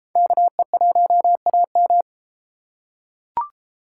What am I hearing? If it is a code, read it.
Morse code, heard as a keyed tone, about 31 wpm, 700 Hz. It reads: KE1AM